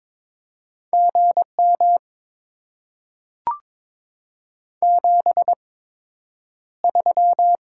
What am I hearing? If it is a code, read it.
Morse code, heard as a keyed tone, about 22 wpm, spaced as for 9 wpm, 700 Hz. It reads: GM 73
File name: Top 20 QSO Elements - 8x Character Spacing - 22wpm